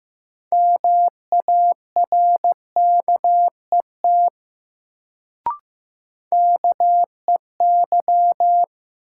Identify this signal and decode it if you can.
Morse code, heard as a keyed tone, about 15 wpm, 700 Hz. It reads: MARKET KEY
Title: Sentences from Top 600 Words - New Word Drill - 15wpm